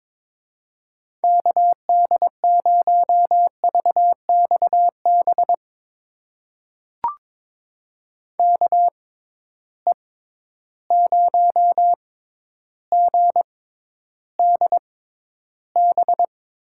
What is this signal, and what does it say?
Morse code, heard as a keyed tone, about 21 wpm, spaced as for 11 wpm, 700 Hz. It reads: KD0VXB KE0GDB
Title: USA Call Signs - 6 Characters - 6x Character Spacing - 22wpm